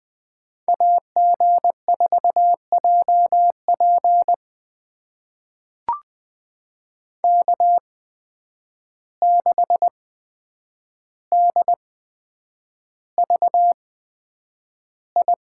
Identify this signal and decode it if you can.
Morse code, heard as a keyed tone, about 20 wpm, spaced as for 9 wpm, 700 Hz. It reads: AG4JP K6DVI
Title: USA Call Signs - 5 Characters - 8x Character Spacing - 20wpm